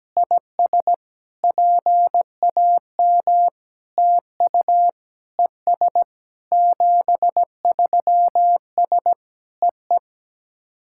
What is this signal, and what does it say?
Morse code, heard as a keyed tone, about 17 wpm, 700 Hz. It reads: IS PAM TU ES 73S EE
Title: SOTA QSO Practice 17wpm